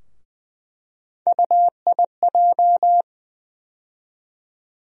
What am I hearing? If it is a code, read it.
Morse code, heard as a keyed tone, about 20 wpm, 700 Hz. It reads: UIJ
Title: USA Call Sign Postfixes - 3 Letters 20wpm